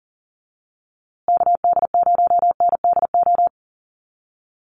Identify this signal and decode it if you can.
Morse code, heard as a keyed tone, about 40 wpm, 700 Hz. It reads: KD0NDO
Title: USA Call Signs Repeated 3x 40wpm